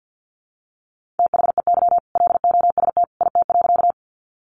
Morse code, heard as a keyed tone, about 50 wpm, 700 Hz. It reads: THEY LOST IT.
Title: Sentences from Top 1400 Words 50wpm